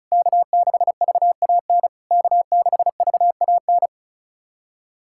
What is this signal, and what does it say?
Morse code, heard as a keyed tone, about 36 wpm, 700 Hz. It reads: K6VAN K6VAN
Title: SOTA QSO Practice 35wpm